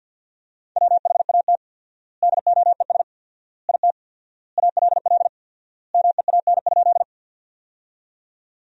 Morse code, heard as a keyed tone, about 49 wpm, 700 Hz. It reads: WHAT DOES IT ALL MEAN?
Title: 4x Word-Spacing - Sentences from Top 200 Words - 50wpm